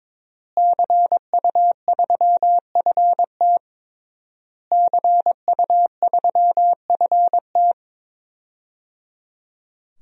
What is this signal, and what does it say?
Morse code, heard as a keyed tone, about 22 wpm, 700 Hz. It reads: CU3FT CU3FT
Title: DXCC-100 Call Signs - 5 Characters - Repeated 3x - 22wpm